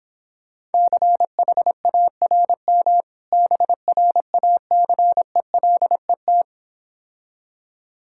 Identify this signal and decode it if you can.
Morse code, heard as a keyed tone, about 26 wpm, 700 Hz. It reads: CHARM BRACELET